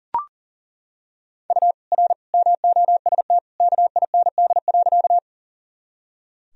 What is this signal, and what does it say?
Morse code, heard as a keyed tone, about 39 wpm, 700 Hz. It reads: U R MOST KIND.
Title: Sentences from Top 200 Words and Abbreviations 40wpm